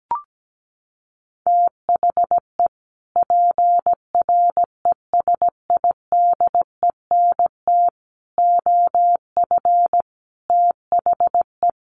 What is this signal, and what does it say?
Morse code, heard as a keyed tone, about 17 wpm, 700 Hz. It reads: THE PRESIDENT OF THE
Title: Sets of 5 Words 17wpm